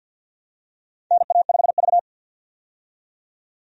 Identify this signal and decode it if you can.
Morse code, heard as a keyed tone, about 52 wpm, 700 Hz. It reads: NA5V
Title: USA Call Signs - 4 Characters - 50wpm